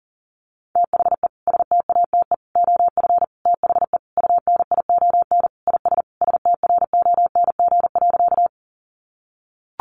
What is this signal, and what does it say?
Morse code, heard as a keyed tone, about 37 wpm, 700 Hz. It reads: THE STATE OF THE UNION IS STRONG.
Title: Sentences from Top 1400 Words 40wpm